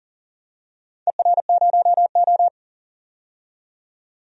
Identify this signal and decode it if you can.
Morse code, heard as a keyed tone, about 39 wpm, 700 Hz. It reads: ER0O
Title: DXCC-100 Call Signs - 4 Characters - 40wpm